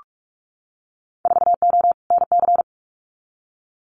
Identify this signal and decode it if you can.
Morse code, heard as a keyed tone, about 46 wpm, 700 Hz. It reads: 4O NC
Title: Winter Field Day Exchanges 45wpm